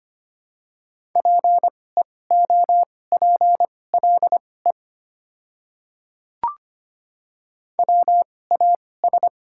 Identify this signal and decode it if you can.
Morse code, heard as a keyed tone, about 25 wpm, spaced as for 18 wpm, 700 Hz. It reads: PEOPLE WAS
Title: 2x Character Spacing - Top 100 Words - 25wpm